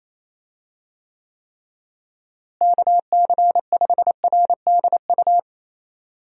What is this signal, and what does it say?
Morse code, heard as a keyed tone, about 28 wpm, 700 Hz. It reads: KC5RDU
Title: USA Call Signs - 6 Characters - Repeated 3x - 28wpm